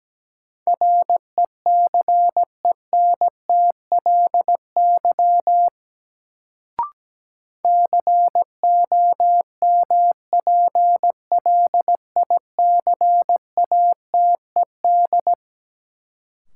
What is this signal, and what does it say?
Morse code, heard as a keyed tone, about 17 wpm, 700 Hz. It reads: RECENTLY COMPLICATED
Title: Sentences from Top 1200 Words - New Word Drill Alt - 17wpm